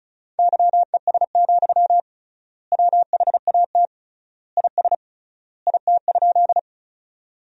Morse code, heard as a keyed tone, about 34 wpm, 700 Hz. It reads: YES, WHAT IS IT?
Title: 3x Word-Spacing - Sentences from Top 500 Words - 35wpm